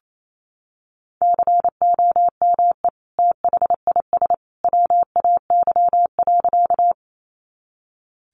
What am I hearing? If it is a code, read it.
Morse code, heard as a keyed tone, about 28 wpm, 700 Hz. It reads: COME THIS WAY.
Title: Sentences from Top 100 Words 28wpm